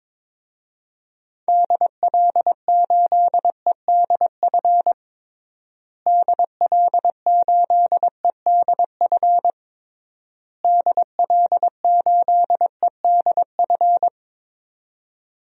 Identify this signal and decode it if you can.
Morse code, heard as a keyed tone, about 22 wpm, 700 Hz. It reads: DL8EDF DL8EDF DL8EDF
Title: DXCC-100 Call Signs - 6 Characters - Repeated 3x - 22wpm